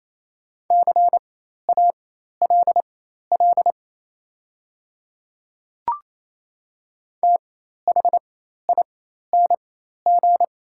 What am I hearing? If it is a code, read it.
Morse code, heard as a keyed tone, about 27 wpm, spaced as for 13 wpm, 700 Hz. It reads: CALL THING
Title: 4x Character Spacing - Top 200 Words - 28wpm